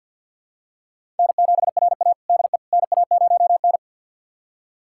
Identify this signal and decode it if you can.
Morse code, heard as a keyed tone, about 51 wpm, 700 Hz. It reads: N7RA DE NA0N
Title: USA Callsign DE Callsign - 4 Characters - 50wpm